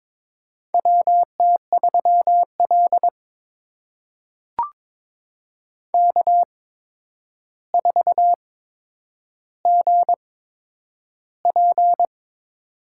Morse code, heard as a keyed tone, about 22 wpm, spaced as for 9 wpm, 700 Hz. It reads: WT3L K4GP
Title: USA Call Signs - 8x Character Spacing - 22wpm